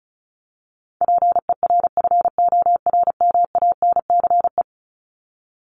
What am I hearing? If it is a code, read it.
Morse code, heard as a keyed tone, about 34 wpm, 700 Hz. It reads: PERFORMANCE
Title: Top 751-1000 Words 35wpm